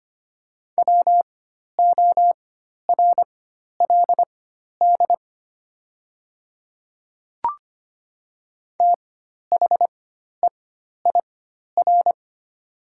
Morse code, heard as a keyed tone, about 25 wpm, spaced as for 12 wpm, 700 Hz. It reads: WORLD THEIR
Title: 4x Character Spacing - Top 200 Words - 25wpm